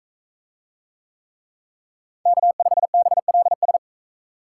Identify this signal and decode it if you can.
Morse code, heard as a keyed tone, about 39 wpm, 700 Hz. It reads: K5BLS